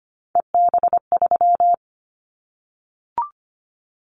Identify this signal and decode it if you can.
Morse code, heard as a keyed tone, about 25 wpm, 700 Hz. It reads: EB3
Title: Callsign Prefixes for Top 25 DXCC - Repeated 3x - 25wpm